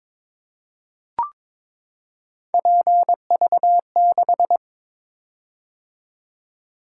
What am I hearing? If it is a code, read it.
Morse code, heard as a keyed tone, about 22 wpm, 700 Hz. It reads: PV6
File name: practice / 22wpm / Three Letter-Number Combinations - 22wpm